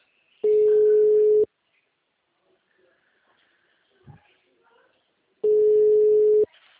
Ringback tone (Europe, 425 Hz).
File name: out- -2024- - - .8899